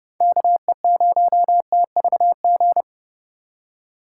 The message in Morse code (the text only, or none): KE0TVG